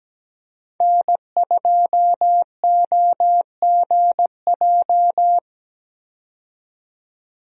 Morse code, heard as a keyed tone, about 17 wpm, 700 Hz. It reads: N2OGJ
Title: USA Call Signs - 5 Characters - 17wpm